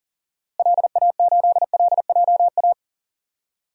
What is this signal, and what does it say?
Morse code, heard as a keyed tone, about 40 wpm, 700 Hz. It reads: LA8LJA
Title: DXCC-100 Call Signs - 6 Characters - Repeated 2x - 40wpm